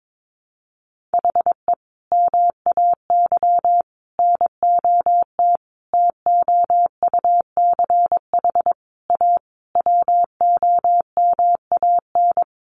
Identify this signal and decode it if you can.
Morse code, heard as a keyed tone, about 22 wpm, 700 Hz. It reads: HE MAY NOT TOUCH A WOMAN
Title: Sentences from Top 1500 Words 22wpm